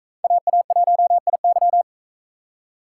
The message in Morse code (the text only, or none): AA1IY